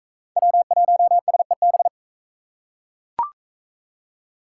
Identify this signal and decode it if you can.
Morse code, heard as a keyed tone, about 42 wpm, 700 Hz. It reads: W1SEB